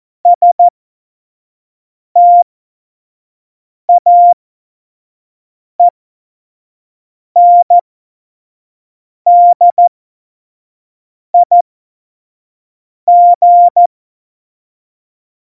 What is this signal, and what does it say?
Morse code, heard as a keyed tone, about 14 wpm, spaced as for 5 wpm, 700 Hz. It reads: STAENDIG
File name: Morsekurs / Witze / Witz 02 Juni 2020, 25 F70